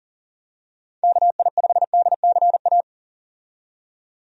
Morse code, heard as a keyed tone, about 40 wpm, 700 Hz. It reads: KI5DCA
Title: USA Call Signs - 6 Characters - 40wpm